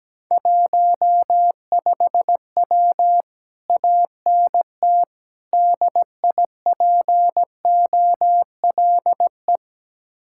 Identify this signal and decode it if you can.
Morse code, heard as a keyed tone, about 17 wpm, 700 Hz. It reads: 15W ANT DIPOLE
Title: Speed-Racing Common QSO Phrases Encore - 17wpm